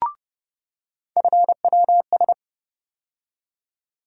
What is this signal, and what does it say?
Morse code, heard as a keyed tone, about 30 wpm, 700 Hz. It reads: FWS
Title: USA Call Sign Postfixes - 3 Letters 30wpm